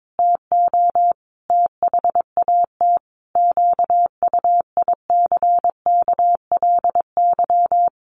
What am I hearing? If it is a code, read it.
Morse code, heard as a keyed tone, about 22 wpm, 700 Hz. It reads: TO THAT QUICKLY
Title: Sentences from Top 700 Words 22wpm